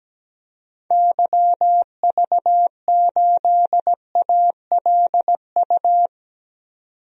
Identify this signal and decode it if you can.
Morse code, heard as a keyed tone, about 17 wpm, 700 Hz. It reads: YV8ALU
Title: DXCC-100 Call Signs - 6 Characters - 17wpm